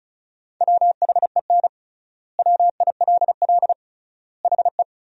Morse code, heard as a keyed tone, about 36 wpm, 700 Hz. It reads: WHEN WILL HE
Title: 3x Word-Spacing - Sentences from Top 200 Words - 35wpm